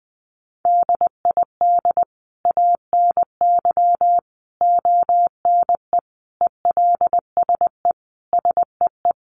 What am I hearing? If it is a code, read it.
Morse code, heard as a keyed tone, about 20 wpm, 700 Hz. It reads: DID ANY ONE ELSE SEE
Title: Sentences from Top 1000 Words 20wpm